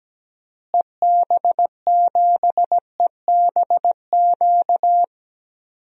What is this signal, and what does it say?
Morse code, heard as a keyed tone, about 17 wpm, 700 Hz. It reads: EB7EBQ
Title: DXCC-100 Call Signs - 6 Characters - Repeated 3x - 17wpm